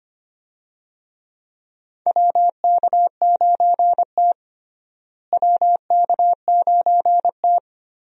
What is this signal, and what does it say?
Morse code, heard as a keyed tone, about 25 wpm, 700 Hz. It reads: WK9T WK9T
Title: USA Call Signs - 4 Characters - Repeated 2x - 25wpm